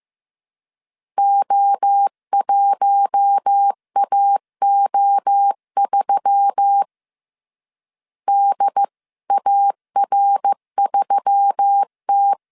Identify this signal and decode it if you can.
Morse code, heard as a keyed tone, about 15 wpm, 800 Hz. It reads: O1AO3 DAR3T
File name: CW / DET / 1 6 1500C S B 14wpm 2958